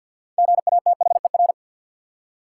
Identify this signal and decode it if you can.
Morse code, heard as a keyed tone, about 48 wpm, 700 Hz. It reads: GATHER
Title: Six Letter Words - 50wpm